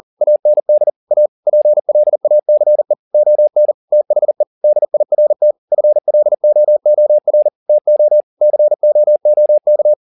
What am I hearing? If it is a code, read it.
Morse code, heard as a keyed tone, about 40 wpm, 600 Hz. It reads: AND A PLACE ON THE DIRT FLOOR TO COOK